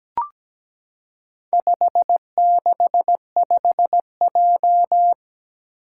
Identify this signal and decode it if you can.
Morse code, heard as a keyed tone, about 17 wpm, 700 Hz. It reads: 565J
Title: Four Challenging Letter-Number Combinations 17wpm